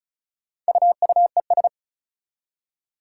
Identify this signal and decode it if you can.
Morse code, heard as a keyed tone, about 36 wpm, 700 Hz. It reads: UUES
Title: Four Challenging Letter-Number Combinations 35wpm